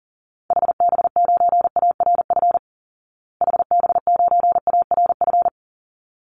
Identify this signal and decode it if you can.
Morse code, heard as a keyed tone, about 40 wpm, 700 Hz. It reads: HB9ARF HB9ARF
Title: Call Signs Repeated 3x - 40wpm